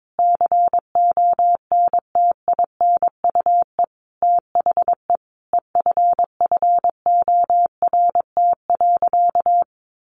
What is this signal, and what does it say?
Morse code, heard as a keyed tone, about 22 wpm, 700 Hz. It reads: CONTINUE THE EFFORT.
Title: Sentences from Top 1200 Words 22wpm